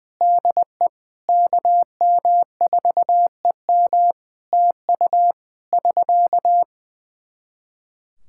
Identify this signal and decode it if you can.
Morse code, heard as a keyed tone, about 20 wpm, 700 Hz. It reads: DE KM4EM TU <SK>
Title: Practice QSO - Exchange 4 - Ending - 20wpm